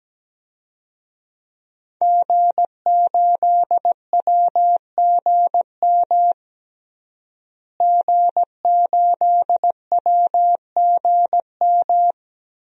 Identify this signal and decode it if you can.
Morse code, heard as a keyed tone, about 17 wpm, 700 Hz. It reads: G8WGM G8WGM
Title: DXCC-100 Call Signs - 5 Characters - Repeated 3x - 17wpm